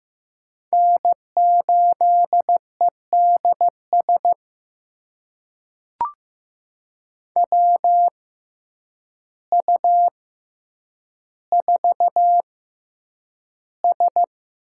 Morse code, heard as a keyed tone, about 15 wpm, spaced as for 8 wpm, 700 Hz. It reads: N8EDS WU4S
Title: USA Call Signs - 6x Character Spacing - 15wpm